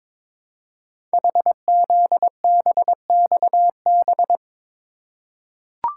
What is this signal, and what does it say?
Morse code, heard as a keyed tone, about 22 wpm, 700 Hz. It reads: HZBXB
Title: Code Groups - 5 Letters - 22wpm